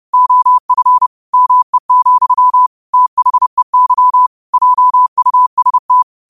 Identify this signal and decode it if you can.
Morse code, heard as a keyed tone, about 30 wpm, 1000 Hz. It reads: OF ME, THEY JUST
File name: qotdfiles / 30wpm